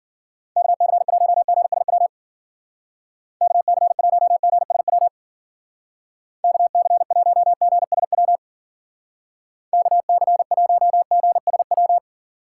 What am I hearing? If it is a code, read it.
Morse code, heard as a keyed tone, about 54 wpm, 700 Hz. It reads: KC1GSW KC1GSW KC1GSW KC1GSW